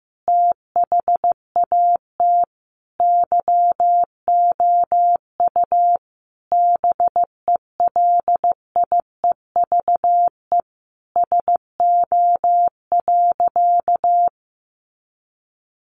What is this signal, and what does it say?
Morse code, heard as a keyed tone, about 15 wpm, 700 Hz. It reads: THAT YOU BELIEVE SO.
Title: Sentences from Top 1000 Words 15wpm